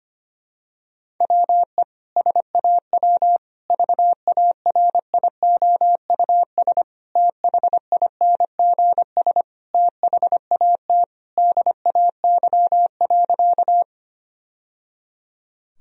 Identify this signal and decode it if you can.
Morse code, heard as a keyed tone, about 25 wpm, 700 Hz. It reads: WE SAW VARIOUS THINGS THAT DAY.
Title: Sentences from Top 1000 Words - Review Part 4 - 25wpm